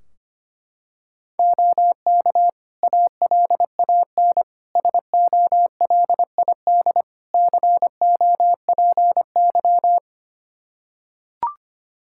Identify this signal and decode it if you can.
Morse code, heard as a keyed tone, about 25 wpm, 700 Hz. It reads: OK ALAN SOLID COPY
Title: Practice QSO - Exchange 3 - More Info - 25wpm